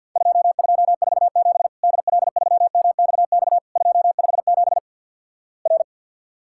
Morse code, heard as a keyed tone, about 48 wpm, 700 Hz. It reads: 2247 DL3MXX 257 R
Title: Worked All Europe DX Contest - QTC Exchange with Repeats - 50wpm